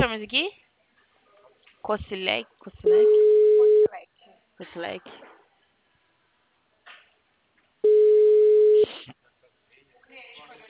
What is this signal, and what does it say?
Ringback tone (Europe, 425 Hz), with speech.